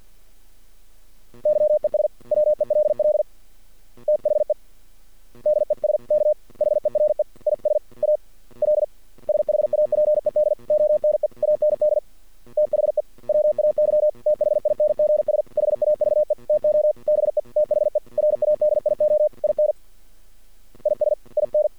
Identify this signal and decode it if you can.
Morse code, heard as a keyed tone, about 48 wpm, 600 Hz. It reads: YES WELL THE DESK HERE IS A L SHAPED ONE AND THE WAY THINGS HAVE TO BE THE RADIO IN IS IN
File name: archived / CRSnet